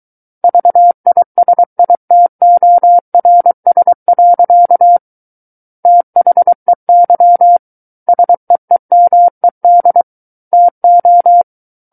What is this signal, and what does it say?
Morse code, heard as a keyed tone, about 23 wpm, 700 Hz. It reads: VISITORS. THEY SEEMED TO
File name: Book One / 14-Chapter 14 - 23 WPM